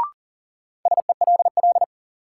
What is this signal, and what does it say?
Morse code, heard as a keyed tone, about 40 wpm, 700 Hz. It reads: SELL
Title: Sentences from Top 1000 Words - Review Part 6 - New Word Drill - 40wpm